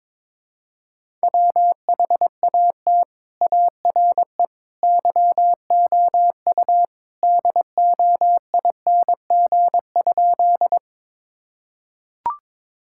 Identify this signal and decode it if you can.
Morse code, heard as a keyed tone, about 21 wpm, 700 Hz. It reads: WHAT ARE YOU DOING?
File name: practice / 22wpm / Sentences with 3-Word Phrases - 22wpm